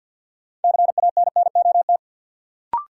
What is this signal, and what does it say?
Morse code, heard as a keyed tone, about 51 wpm, 700 Hz. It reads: CANNOT
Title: Sentences from Top 1000 Words - Review Part 6 - New Word Drill - 50wpm